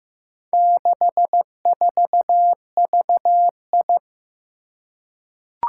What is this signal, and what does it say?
Morse code, heard as a keyed tone, about 15 wpm, 700 Hz. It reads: 64VI
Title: Four Challenging Letter-Number Combinations 15wpm